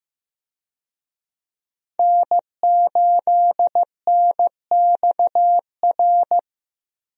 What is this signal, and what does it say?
Morse code, heard as a keyed tone, about 15 wpm, 700 Hz. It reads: N8NXR
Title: USA Call Signs - 5 Characters - 15wpm